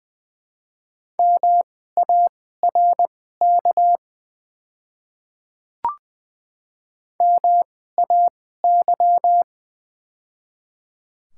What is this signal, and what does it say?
Morse code, heard as a keyed tone, about 20 wpm, spaced as for 15 wpm, 700 Hz. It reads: MARK MAY
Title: 2x Character Spacing - Top 300 Words - 20wpm